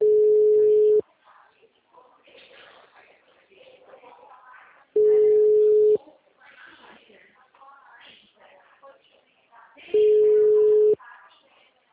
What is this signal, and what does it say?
Ringback tone (Europe, 425 Hz), with speech.